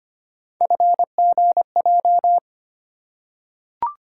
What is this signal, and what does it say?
Morse code, heard as a keyed tone, about 25 wpm, 700 Hz. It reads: FGJ
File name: Three Letter-Number Combinations - 25wpm